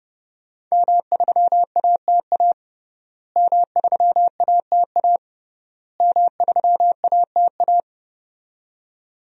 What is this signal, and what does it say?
Morse code, heard as a keyed tone, about 30 wpm, 700 Hz. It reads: M3ATA M3ATA M3ATA
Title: DXCC-100 Call Signs - 5 Characters - Repeated 3x - 30wpm